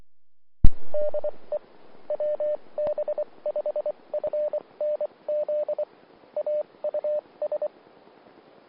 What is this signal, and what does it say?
Morse code, heard as a keyed tone, about 24 wpm, spaced as for 20 wpm, 600 Hz. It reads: DE WB5FNZ AUS
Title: Austin, TX Amateur Repeaters